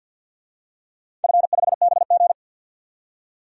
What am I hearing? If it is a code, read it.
Morse code, heard as a keyed tone, about 51 wpm, 700 Hz. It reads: F5BG
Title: DXCC-100 Call Signs - 4 Characters - 50wpm